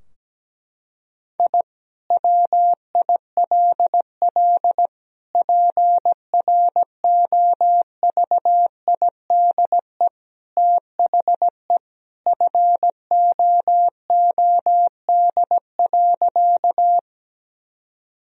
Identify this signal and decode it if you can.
Morse code, heard as a keyed tone, about 17 wpm, 700 Hz. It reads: I WILL PROVIDE THE FOOD.
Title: Sentences from Top 1000 Words - Review Part 3 - 17wpm